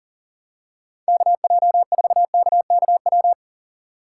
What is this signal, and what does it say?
Morse code, heard as a keyed tone, about 40 wpm, 700 Hz. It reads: KJ4KKW
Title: USA Call Signs - 6 Characters - 4x Character Spacing - 40wpm